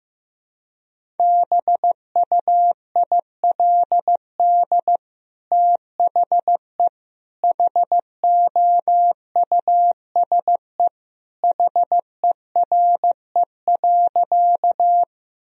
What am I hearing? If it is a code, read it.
Morse code, heard as a keyed tone, about 15 wpm, 700 Hz. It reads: BUILD THE HOUSE HERE.
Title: Sentences from Top 1000 Words - Review Part 1 - 15wpm